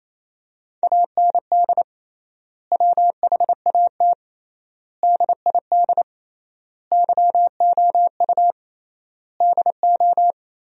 Morse code, heard as a keyed tone, about 28 wpm, 700 Hz. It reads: AND WHAT DID YOU DO
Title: 3x Word-Spacing - Sentences from Top 300 Words - 28wpm